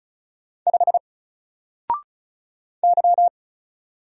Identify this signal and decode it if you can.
Morse code, heard as a keyed tone, about 34 wpm, 700 Hz. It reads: 5 Y
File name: Single-Letter-Number-35wpm